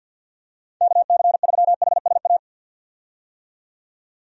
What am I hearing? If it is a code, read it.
Morse code, heard as a keyed tone, about 51 wpm, 700 Hz. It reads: KC3HSA
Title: USA Call Signs - 6 Characters - 50wpm